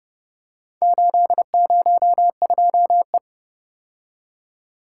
Morse code, heard as a keyed tone, about 29 wpm, 700 Hz. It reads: 802E